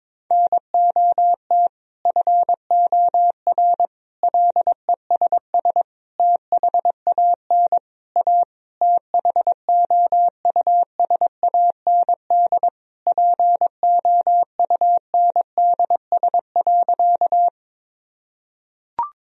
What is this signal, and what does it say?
Morse code, heard as a keyed tone, about 22 wpm, 700 Hz. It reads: NOT FOR LESS THAN A THOUSAND POUNDS.
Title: Sentences from Top 1000 Words - Review Part 3 - 22wpm